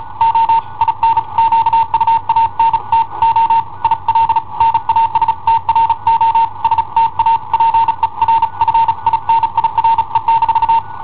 Morse code, heard as a keyed tone, about 36 wpm, 910 Hz. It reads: O IN QUANTO IL NASTRO STA PER FINIRE-
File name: i7ohp / cassetta i7ohp B